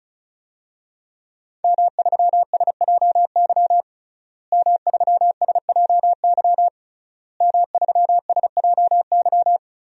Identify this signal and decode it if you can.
Morse code, heard as a keyed tone, about 34 wpm, 700 Hz. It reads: M3SJY M3SJY M3SJY
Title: DXCC-100 Call Signs - 5 Characters - Repeated 3x - 35wpm